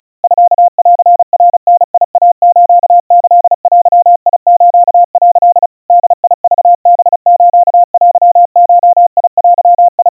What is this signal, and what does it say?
Morse code (Japanese Wabun code), heard as a keyed tone, about 35 wpm, 700 Hz. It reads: ミンナダイスキデス」 ボクハステゴデ
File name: kawa / Morse sample / kamyu 130 cpm 700 Hz Crispy